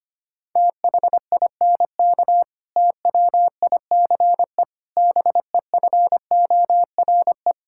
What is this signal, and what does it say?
Morse code, heard as a keyed tone, about 25 wpm, 700 Hz. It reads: THINK TWICE BEFORE
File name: practice / 25wpm / Difficult Sentences with Hints - 25wpm